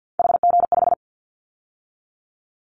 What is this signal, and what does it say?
Morse code, heard as a keyed tone, about 50 wpm, 700 Hz. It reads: HG5
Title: Callsign Prefixes for Top 50 DXCC - 50wpm